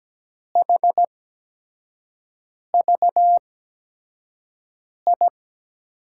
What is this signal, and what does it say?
Morse code, heard as a keyed tone, about 17 wpm, spaced as for 5 wpm, 700 Hz. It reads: HVI
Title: USA Call Signs - 5 Characters - 8x Character Spacing - 17wpm